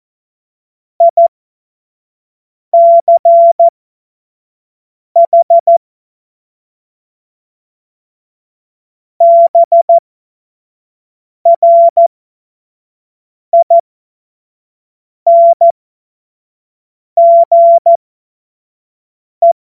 Morse code, heard as a keyed tone, about 14 wpm, spaced as for 5 wpm, 680 Hz. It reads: ICH BRINGE
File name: Morsekurs / Witze / Witz 06 Februar 2024 25 F70